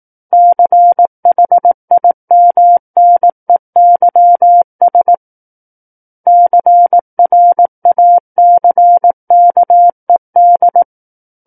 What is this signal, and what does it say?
Morse code, heard as a keyed tone, about 18 wpm, 700 Hz. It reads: CHIMNEYS CRACKED
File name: Book One / 09-Chapter 9 - 18 WPM